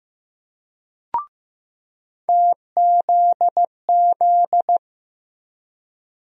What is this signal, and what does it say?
Morse code, heard as a keyed tone, about 15 wpm, 700 Hz. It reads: TZZ
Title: USA Call Sign Postfixes - 3 Letters 15wpm